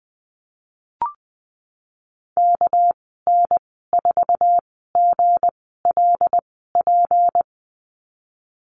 Morse code, heard as a keyed tone, about 20 wpm, spaced as for 14 wpm, 700 Hz. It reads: KN4GLP
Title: 245 Lesson 39g - Letter z - Call Signs 20wpm